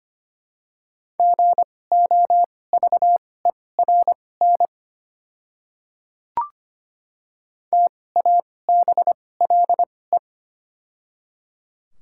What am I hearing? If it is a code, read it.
Morse code, heard as a keyed tone, about 25 wpm, spaced as for 18 wpm, 700 Hz. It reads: GOVERN TABLE
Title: 2x Character Spacing - Top 500 Words - 25wpm